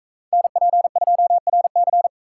Morse code, heard as a keyed tone, about 41 wpm, 700 Hz. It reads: NP2RC